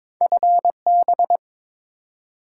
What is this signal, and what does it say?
Morse code, heard as a keyed tone, about 22 wpm, 690 Hz. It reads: FB
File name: Top 20 QSO Elements - 2x Character Spacing - 22wpm